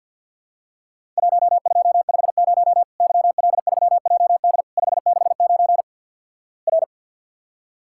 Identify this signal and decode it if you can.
Morse code, heard as a keyed tone, about 50 wpm, 700 Hz. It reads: 1250 YL3JD 569 R
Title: Worked All Europe DX Contest - QTC Exchange with Repeats - 50wpm